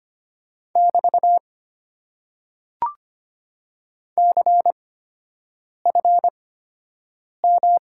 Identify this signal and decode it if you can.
Morse code, heard as a keyed tone, about 25 wpm, spaced as for 7 wpm, 690 Hz. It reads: <BT> CFM
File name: Top 90 QSO Elements - 8x Character Spacing - 25wpm